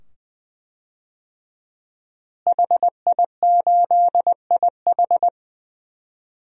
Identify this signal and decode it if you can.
Morse code, heard as a keyed tone, about 20 wpm, 700 Hz. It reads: HI8IH
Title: DXCC-100 Call Signs - 5 Characters - Repeated 2x - 20wpm